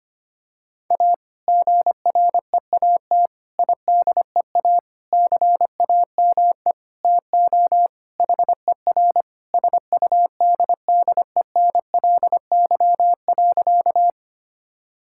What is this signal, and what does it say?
Morse code, heard as a keyed tone, about 25 wpm, 700 Hz. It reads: A GREAT IDEA CAME TO HER SUDDENLY.
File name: practice / 25wpm / Sentences with 2-Word Phrases - 25wpm